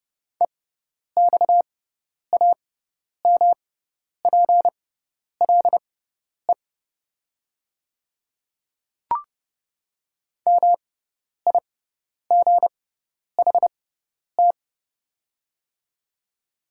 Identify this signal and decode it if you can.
Morse code, heard as a keyed tone, about 30 wpm, spaced as for 10 wpm, 700 Hz. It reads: EXAMPLE MIGHT
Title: 6x Character Spacing - Top 500 Words - 30wpm